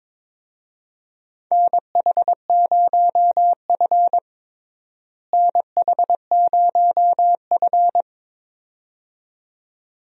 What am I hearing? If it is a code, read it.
Morse code, heard as a keyed tone, about 22 wpm, 700 Hz. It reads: NH0F NH0F
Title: USA Call Signs - 4 Characters - Repeated 2x - 22wpm